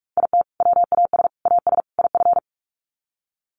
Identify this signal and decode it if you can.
Morse code, heard as a keyed tone, about 44 wpm, 700 Hz. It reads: IT WAS AS IF
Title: Sets of 4 Words 45wpm